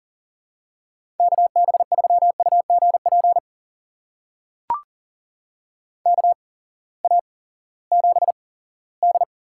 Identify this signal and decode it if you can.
Morse code, heard as a keyed tone, about 40 wpm, spaced as for 20 wpm, 700 Hz. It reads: KB3UGP KA7D
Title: USA Call Signs - 8x Character Spacing - 40wpm